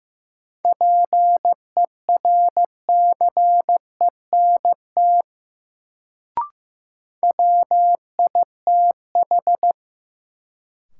Morse code, heard as a keyed tone, about 16 wpm, 700 Hz. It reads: PERCENT WITH
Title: Sets of 2 Words - New Word Drill - 15wpm